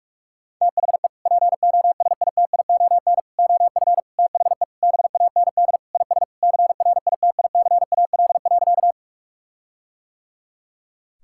Morse code, heard as a keyed tone, about 46 wpm, 700 Hz. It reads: THE POSITION OF THE BAND IS CRITICAL.
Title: Sentences from Top 1600 Words 45wpm